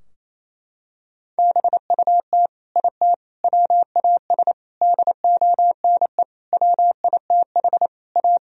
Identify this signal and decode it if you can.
Morse code, heard as a keyed tone, about 28 wpm, 700 Hz. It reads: BUT IT WAS DONE WITH A
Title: Sentences from Top 1000 Words - Review Part 4 - 28wpm